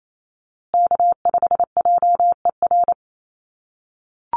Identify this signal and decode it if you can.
Morse code, heard as a keyed tone, about 28 wpm, 700 Hz. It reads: K5JER
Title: USA Call Signs Repeated 3x 28wpm